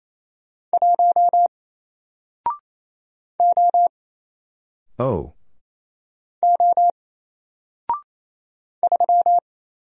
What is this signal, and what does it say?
Morse code, heard as a keyed tone, about 28 wpm, 700 Hz. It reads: 1 O O 3
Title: Single Letter-Number Alt 28wpm